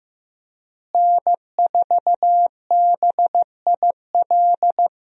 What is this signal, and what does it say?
Morse code, heard as a keyed tone, about 15 wpm, 700 Hz. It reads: N4BIL
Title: USA Call Signs - 5 Characters - Repeated 2x - 15wpm